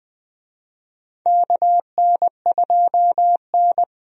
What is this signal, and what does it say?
Morse code, heard as a keyed tone, about 21 wpm, 700 Hz. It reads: KN2N
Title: USA Call Signs - 4 Characters - Repeated 3x - 20wpm